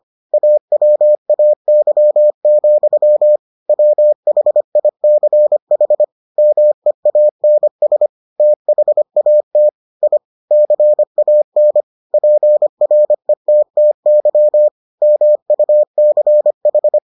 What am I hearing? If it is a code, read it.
Morse code, heard as a keyed tone, about 25 wpm, 600 Hz. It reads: AWAY, WHICH MEANS THAT I CAN PRETTY MUCH